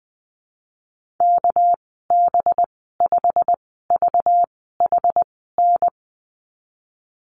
Morse code, heard as a keyed tone, about 20 wpm, spaced as for 14 wpm, 700 Hz. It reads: KB5VHN
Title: 231 Lesson 37g - Letter x - Call Signs 20wpm